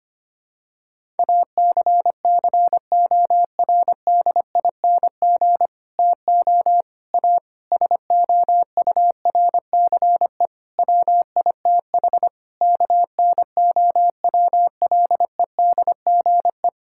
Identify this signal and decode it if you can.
Morse code, heard as a keyed tone, about 25 wpm, 700 Hz. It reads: ACCORDING TO A SOURCE WITH KNOWLEDGE
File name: practice / 25wpm / Sentences with 9-Word Phrases - 25wpm